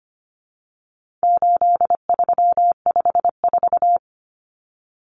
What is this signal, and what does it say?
Morse code, heard as a keyed tone, about 25 wpm, 700 Hz. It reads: 8354
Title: Numbers - 4-Digit 25wpm